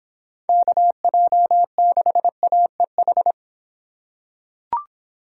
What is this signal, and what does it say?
Morse code, heard as a keyed tone, about 26 wpm, 700 Hz. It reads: KJ6AEH